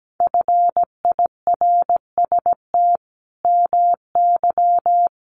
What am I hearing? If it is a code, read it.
Morse code, heard as a keyed tone, about 17 wpm, 700 Hz. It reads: FIRST MY
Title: Sentences from Top 1200 Words 17wpm